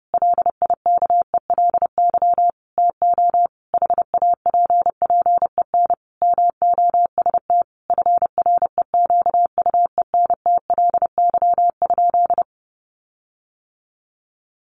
Morse code, heard as a keyed tone, about 30 wpm, 700 Hz. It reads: LIKELY TO HAPPEN MOST FREQUENTLY?
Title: Sentences from Top 1300 Words 30wpm